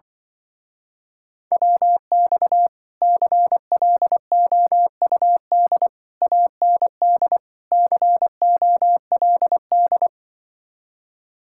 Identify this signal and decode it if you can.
Morse code, heard as a keyed tone, about 24 wpm, 700 Hz. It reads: WX CLOUD AND COLD
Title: Speed-Racing Common QSO Phrases - 20wpm